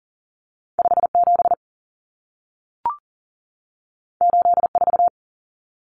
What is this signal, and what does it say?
Morse code, heard as a keyed tone, about 40 wpm, 700 Hz. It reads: 57 84
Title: Common Cut Numbers - 2-Digit 40wpm